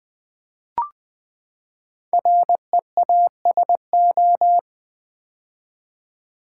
Morse code, heard as a keyed tone, about 20 wpm, 700 Hz. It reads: REASO
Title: Build A Word - 5 Letters At A Time - 10-Letter Words - 20wpm